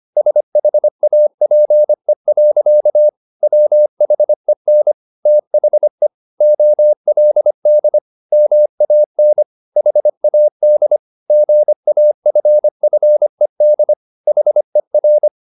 Morse code, heard as a keyed tone, about 25 wpm, 600 Hz. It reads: SHAPE. WHEN THE OLD MAN HAD GAFFED HER
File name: the old man and the sea part 1 25wpm